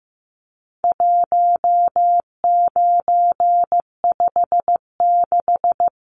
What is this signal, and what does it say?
Morse code, heard as a keyed tone, about 15 wpm, 700 Hz. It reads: 1956
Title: Numbers - 4-Digit 15wpm